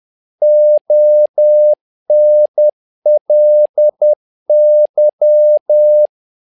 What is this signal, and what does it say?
Morse code, heard as a keyed tone, about 10 wpm, 600 Hz. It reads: ONLY
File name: learning center / tower trouble / tower trouble 10wpm